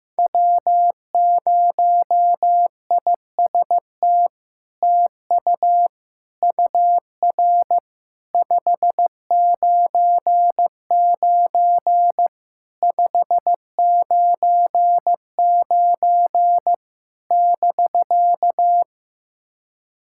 Morse code, heard as a keyed tone, about 15 wpm, 700 Hz. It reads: W0IST TU UR 599 599 <BK>
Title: SOTA QSO Practice 15wpm